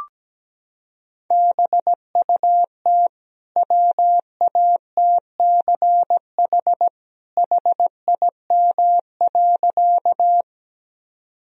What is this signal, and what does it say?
Morse code, heard as a keyed tone, about 17 wpm, 700 Hz. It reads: BUT WATCH HIM.
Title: Sentences from Top 1000 Words - Review Part 1 - 17wpm